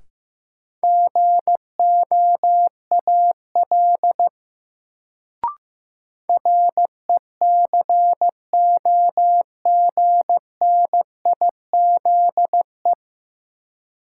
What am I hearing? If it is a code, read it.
Morse code, heard as a keyed tone, about 15 wpm, 700 Hz. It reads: GOAL RECOGNIZE
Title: Sentences from Top 1400 Words - New Word Drill Alt - 15wpm